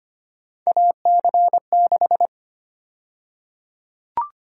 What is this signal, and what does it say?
Morse code, heard as a keyed tone, about 25 wpm, 700 Hz. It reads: AC6
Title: USA Call Sign Prefixes - Repeated 2x - 25wpm